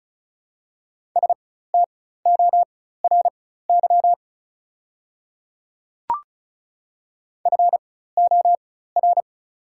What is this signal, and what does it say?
Morse code, heard as a keyed tone, about 34 wpm, spaced as for 16 wpm, 690 Hz. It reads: STORY FOR
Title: 4x Character Spacing - Top 300 Words - 35wpm